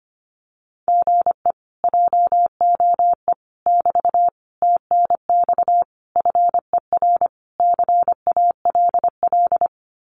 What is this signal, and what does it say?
Morse code, heard as a keyed tone, about 25 wpm, 700 Hz. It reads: GE JOE <BT> TNX FER CALL
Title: Common QSO Phrases Encore 25wpm